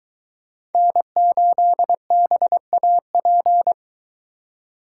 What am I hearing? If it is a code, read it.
Morse code, heard as a keyed tone, about 22 wpm, 700 Hz. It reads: N8BAP